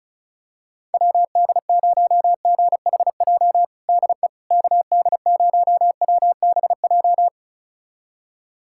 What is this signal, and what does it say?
Morse code, heard as a keyed tone, about 36 wpm, 700 Hz. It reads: WD0GHJ DE KD0WBJ
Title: USA Callsign DE Callsign - 6 Characters - 35wpm